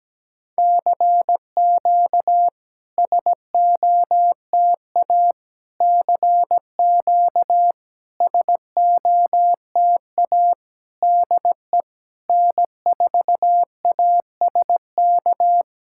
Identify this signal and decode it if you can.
Morse code, heard as a keyed tone, about 17 wpm, 700 Hz. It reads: CQ SOTA CQ SOTA DE N4ASK
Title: SOTA QSO Practice 17wpm